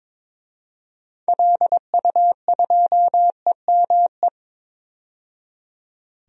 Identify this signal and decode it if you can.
Morse code, heard as a keyed tone, about 22 wpm, 700 Hz. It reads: LU2EME
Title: DXCC-100 Call Signs - 6 Characters - 22wpm